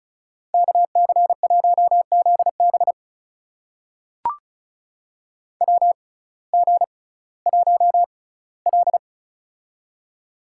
Morse code, heard as a keyed tone, about 34 wpm, spaced as for 19 wpm, 700 Hz. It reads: KC1ZB WG1L
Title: USA Call Signs - 6x Character Spacing - 35wpm